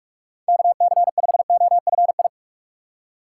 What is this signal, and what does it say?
Morse code, heard as a keyed tone, about 46 wpm, 700 Hz. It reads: KC5OFI
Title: USA Call Signs - 6 Characters - 2x Character Spacing - 45wpm